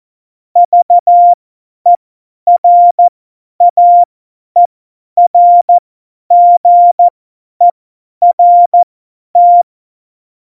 Morse code, heard as a keyed tone, about 14 wpm, spaced as for 10 wpm, 700 Hz. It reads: VERAERGERT
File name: Morsekurs / Witze / Witz 13. August 2019, 50 F70